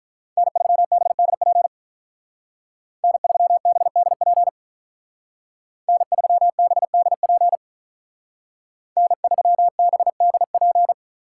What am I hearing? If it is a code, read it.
Morse code, heard as a keyed tone, about 44 wpm, 700 Hz. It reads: N3BDP N3BDP N3BDP N3BDP